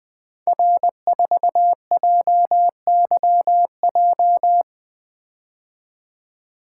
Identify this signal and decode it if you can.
Morse code, heard as a keyed tone, about 20 wpm, 700 Hz. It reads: R4JYJ
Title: Code Groups - 5 Letter:Numbers - 20wpm